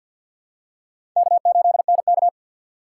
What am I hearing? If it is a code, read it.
Morse code, heard as a keyed tone, about 53 wpm, 700 Hz. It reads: K8NK
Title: USA Call Signs - 4 Characters - 6x Character Spacing - 50wpm